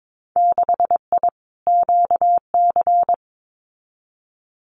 Morse code, heard as a keyed tone, about 22 wpm, 700 Hz. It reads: 6I QC
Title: Winter Field Day Exchanges Encore 22wpm